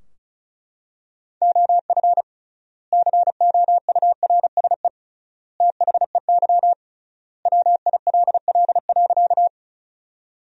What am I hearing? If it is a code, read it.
Morse code, heard as a keyed tone, about 34 wpm, 700 Hz. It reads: OF COURSE THEY WILL.
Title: 3x Word-Spacing - Sentences from Top 500 Words - 35wpm